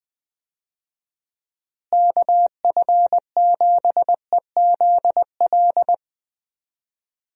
Morse code, heard as a keyed tone, about 21 wpm, 700 Hz. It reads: KF7EZL
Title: USA Call Signs - 6 Characters - Repeated 2x - 20wpm